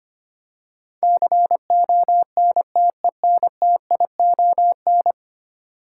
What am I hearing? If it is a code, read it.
Morse code, heard as a keyed tone, about 25 wpm, 700 Hz. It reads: CONTENTION
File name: 10 Letter Words - First 3 Letters Given - 25wpm